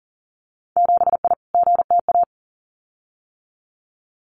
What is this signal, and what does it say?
Morse code, heard as a keyed tone, about 40 wpm, 700 Hz. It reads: 7I GTA
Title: Winter Field Day Exchanges Encore 40wpm